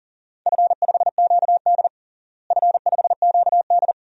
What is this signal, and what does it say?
Morse code, heard as a keyed tone, about 40 wpm, 700 Hz. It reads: F5QD F5QD
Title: DXCC-100 Call Signs - 4 Characters - Repeated 2x - 40wpm